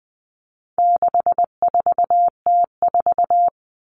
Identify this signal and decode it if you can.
Morse code, heard as a keyed tone, about 20 wpm, 700 Hz. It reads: 64T4
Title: All Cut Numbers - 4-Digit 20wpm